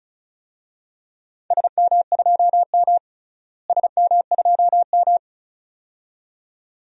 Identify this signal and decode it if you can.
Morse code, heard as a keyed tone, about 34 wpm, 700 Hz. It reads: SM2M SM2M
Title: DXCC-100 Call Signs - 4 Characters - Repeated 2x - 35wpm